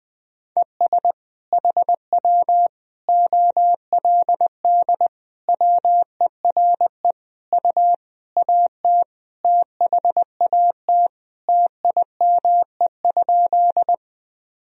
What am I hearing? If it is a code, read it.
Morse code, heard as a keyed tone, about 20 wpm, 700 Hz. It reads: ES HW OLD WERE U AT THAT TIME?
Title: Sentences from Top 200 Words and Abbreviations 20wpm